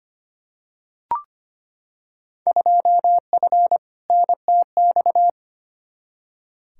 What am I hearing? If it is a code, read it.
Morse code, heard as a keyed tone, about 25 wpm, 700 Hz. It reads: 2F NTX
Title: Field Day Exchanges - 25wpm